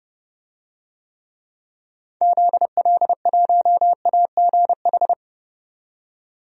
Morse code, heard as a keyed tone, about 30 wpm, 700 Hz. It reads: ZL1AGH
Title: DXCC-100 Call Signs - 6 Characters - 30wpm